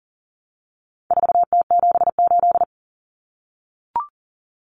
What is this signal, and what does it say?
Morse code, heard as a keyed tone, about 40 wpm, 700 Hz. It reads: 4T78
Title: Common Cut Numbers - 4-Digit 40wpm